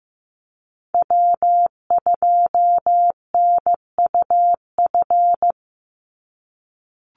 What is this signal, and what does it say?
Morse code, heard as a keyed tone, about 15 wpm, 700 Hz. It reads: W2NUF
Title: USA Call Signs 15wpm